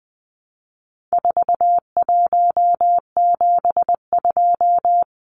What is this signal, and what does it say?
Morse code, heard as a keyed tone, about 20 wpm, 700 Hz. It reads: 4172
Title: Numbers - 4-Digit 20wpm